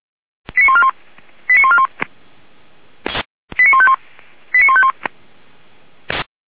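Five-tone selective call (ZVEI), heard as quick runs of five tones, tones of about 70 ms. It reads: 89141 89141 89151 89151
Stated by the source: ZVEI five-tone call received via SDR.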